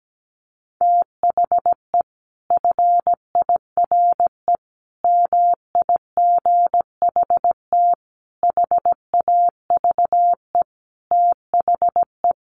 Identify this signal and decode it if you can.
Morse code, heard as a keyed tone, about 17 wpm, 700 Hz. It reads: THE FIRE MIGHT HAVE THE
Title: Sentences from Top 1300 Words 17wpm